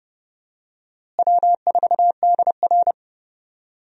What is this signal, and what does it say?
Morse code, heard as a keyed tone, about 30 wpm, 700 Hz. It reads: W4DR
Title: USA Call Signs - 4 Characters - Repeated 2x - 30wpm